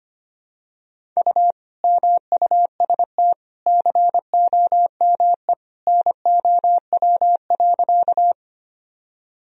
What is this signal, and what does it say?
Morse code, heard as a keyed tone, about 25 wpm, 700 Hz. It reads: U MUST COME NOW.
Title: Sentences from Top 200 Words and Abbreviations 25wpm